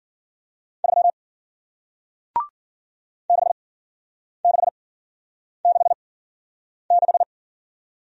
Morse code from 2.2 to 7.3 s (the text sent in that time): 6666